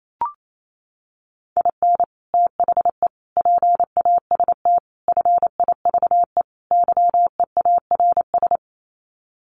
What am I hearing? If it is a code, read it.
Morse code, heard as a keyed tone, about 28 wpm, 700 Hz. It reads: IN THE PAST FIVE YEARS
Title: Sets of 5 Words Encore 28wpm